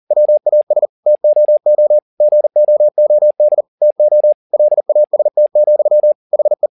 Morse code, heard as a keyed tone, about 40 wpm, 600 Hz. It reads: WAS TOO GOOD TO LAST, HE
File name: the old man and the sea 2 40wpm